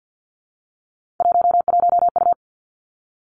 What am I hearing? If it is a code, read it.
Morse code, heard as a keyed tone, about 50 wpm, 700 Hz. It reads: 11U
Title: Three Challenging Letter-Number Combinations 50wpm